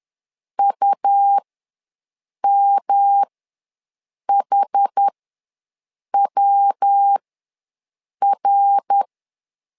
Morse code, heard as a keyed tone, about 11 wpm, spaced as for 6 wpm, 800 Hz. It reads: UMHWR
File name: LEZIONI / 3 MWRO / 3 MWR ws 10wpm 2151